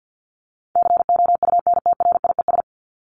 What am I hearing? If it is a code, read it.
Morse code, heard as a keyed tone, about 50 wpm, 700 Hz. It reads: COUNTRIES
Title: ES:S Suffix - Top 100 Words with Base Words 50wpm